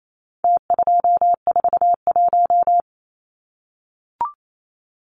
Morse code, heard as a keyed tone, about 28 wpm, 700 Hz. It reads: T241
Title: Numbers - Contest Style - 4-Digit 28wpm